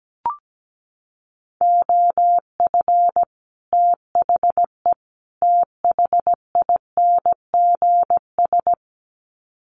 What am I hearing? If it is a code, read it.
Morse code, heard as a keyed tone, about 17 wpm, 700 Hz. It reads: OF THE THINGS
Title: Sets of 3 Words 17wpm